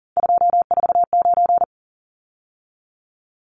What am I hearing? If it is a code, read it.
Morse code, heard as a keyed tone, about 40 wpm, 700 Hz. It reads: 249
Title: Numbers - 3-Digit 40wpm